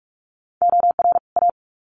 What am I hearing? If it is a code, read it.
Morse code, heard as a keyed tone, about 44 wpm, 700 Hz. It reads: OR A
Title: Sets of 2 Words 45wpm